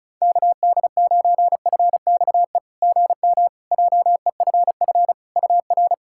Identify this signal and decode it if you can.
Morse code, heard as a keyed tone, about 36 wpm, 690 Hz. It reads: KD9FXE GM JEFF UR
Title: SOTA QSO Practice 35wpm